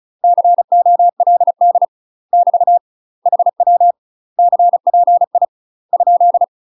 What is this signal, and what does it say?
Morse code, heard as a keyed tone, about 34 wpm, 700 Hz. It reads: COLD <BT> HW CPI ?
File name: UK MASTER QSOs TRAINING / 35WPM / UK MASTER QSOs 0024